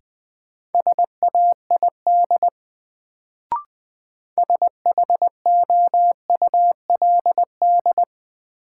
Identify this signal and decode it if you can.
Morse code, heard as a keyed tone, about 20 wpm, 700 Hz. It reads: SAID SHOULD
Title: Sets of 2 Words - New Word Drill - 20wpm